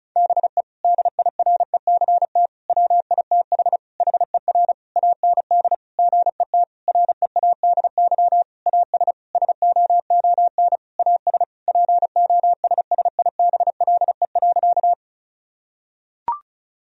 Morse code, heard as a keyed tone, about 36 wpm, 700 Hz. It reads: BE DIRECT WITH HER AND GET READY AS SOON AS POSSIBLE.
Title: Sentences from Top 1000 Words - Review Part 2 - 35wpm